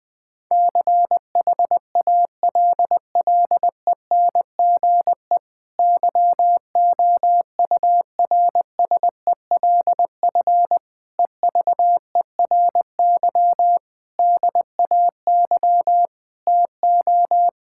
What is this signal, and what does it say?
Morse code, heard as a keyed tone, about 20 wpm, 700 Hz. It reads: CHALLENGE YOURSELF EVERY DAY TO
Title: Difficult Sentences with Hints - 20wpm